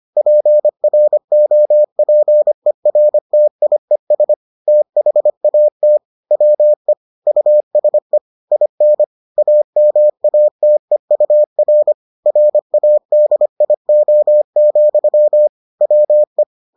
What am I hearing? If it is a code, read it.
Morse code, heard as a keyed tone, about 25 wpm, 600 Hz. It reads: PROPERTIES THAT WE USE IN AMATEUR RADIO, WE